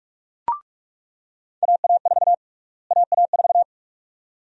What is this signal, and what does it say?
Morse code, heard as a keyed tone, about 46 wpm, 700 Hz. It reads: AA4 AA4
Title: USA Call Sign Prefixes - Repeated 2x - 45wpm